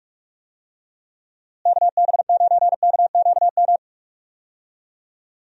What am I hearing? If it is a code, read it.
Morse code, heard as a keyed tone, about 46 wpm, 700 Hz. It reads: KB9KQM
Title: USA Call Signs - 6 Characters - Repeated 2x - 45wpm